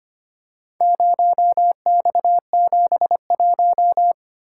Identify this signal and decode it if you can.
Morse code, heard as a keyed tone, about 25 wpm, 700 Hz. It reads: 0X71
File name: practice / 25wpm / Four Letter-Number Combinations - 25wpm